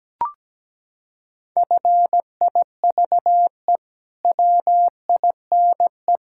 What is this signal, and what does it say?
Morse code, heard as a keyed tone, about 17 wpm, 700 Hz. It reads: FIVE WINE
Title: Pangram Challenge v2 - 17wpm